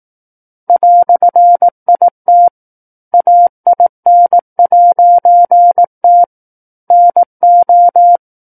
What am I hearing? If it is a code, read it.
Morse code, heard as a keyed tone, about 18 wpm, 700 Hz. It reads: "IT AIN'T NO
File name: 18 WPM